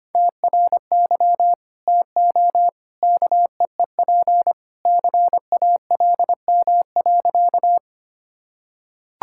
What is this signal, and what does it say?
Morse code, heard as a keyed tone, about 25 wpm, 700 Hz. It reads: TRY TO KEEP CALM.
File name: practice / 25wpm / Sentences with 2-Word Phrases - 25wpm